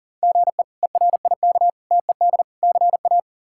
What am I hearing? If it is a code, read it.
Morse code, heard as a keyed tone, about 40 wpm, 700 Hz. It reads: GE ERIK TED CA
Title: K1USN SST QSO Practice - 40wpm